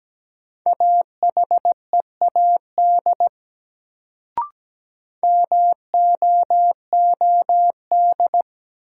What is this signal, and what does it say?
Morse code, heard as a keyed tone, about 17 wpm, 700 Hz. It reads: AHEAD MOOD
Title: Sentences from Top 1100 Words - New Word Drill Alt - 17wpm